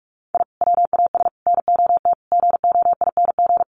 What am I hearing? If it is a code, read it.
Morse code, heard as a keyed tone, about 44 wpm, 700 Hz. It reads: I WAS NOT GOING
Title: Sets of 5 Words 45wpm